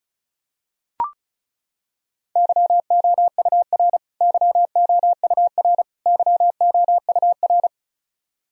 Morse code, heard as a keyed tone, about 36 wpm, 700 Hz. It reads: YOUR YOUR YOUR
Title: Top 100 Words Repeated 35wpm